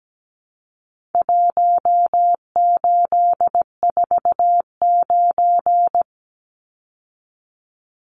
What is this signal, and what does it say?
Morse code, heard as a keyed tone, about 17 wpm, 700 Hz. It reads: 1849
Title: Numbers - Contest Style - 4-Digit 17wpm